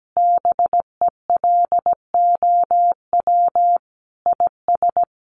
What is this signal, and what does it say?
Morse code, heard as a keyed tone, about 17 wpm, 700 Hz. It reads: BELOW IS
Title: Sentences from Top 1500 Words 17wpm